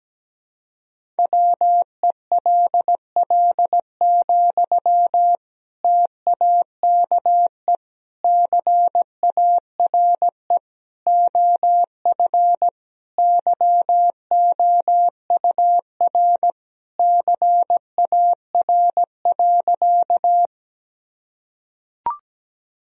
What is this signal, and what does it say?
Morse code, heard as a keyed tone, about 17 wpm, 700 Hz. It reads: WELL, TAKE CARE OF YOUR CAR.
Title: Sentences from Top 1000 Words - Review Part 1 - 17wpm